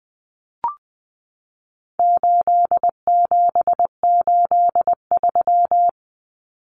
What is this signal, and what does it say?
Morse code, heard as a keyed tone, about 20 wpm, 700 Hz. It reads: 8783
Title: Common Cut Numbers - 4-Digit 20wpm